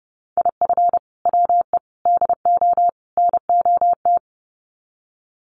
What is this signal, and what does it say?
Morse code, heard as a keyed tone, about 30 wpm, 700 Hz. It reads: IF WE DO NOT
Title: Sets of 4 Words 30wpm